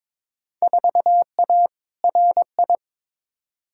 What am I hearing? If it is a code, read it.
Morse code, heard as a keyed tone, about 22 wpm, 700 Hz. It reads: 4A RI
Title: Field Day Exchanges Encore - 22wpm